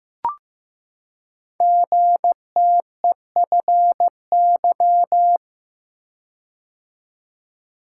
Morse code, heard as a keyed tone, about 15 wpm, 700 Hz. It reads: GTEFY
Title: Code Groups - 5 Letter:Numbers - 15wpm